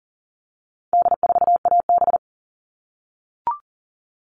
Morse code, heard as a keyed tone, about 40 wpm, 700 Hz. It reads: D4AB
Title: All Cut Numbers - 4-Digit 40wpm